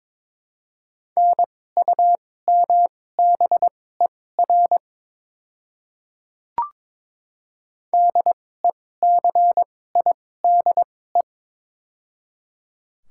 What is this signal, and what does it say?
Morse code, heard as a keyed tone, about 22 wpm, spaced as for 16 wpm, 700 Hz. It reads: NUMBER DECIDE
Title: 2x Character Spacing - Top 500 Words - 22wpm